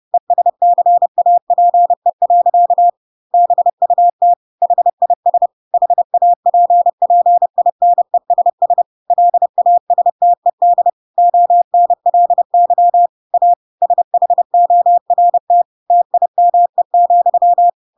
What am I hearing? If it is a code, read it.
Morse code, heard as a keyed tone, about 30 wpm, 700 Hz. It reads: ESCAPE. BUT HIS HAPPINESS LASTED ONLY A SHORT TIME,
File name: Pinocchio / Pinocchio ch4 30 30 ik1hge